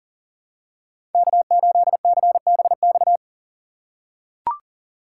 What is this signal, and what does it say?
Morse code, heard as a keyed tone, about 39 wpm, 700 Hz. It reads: K8CBX
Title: USA Call Signs - 5 Characters - 4x Character Spacing - 40wpm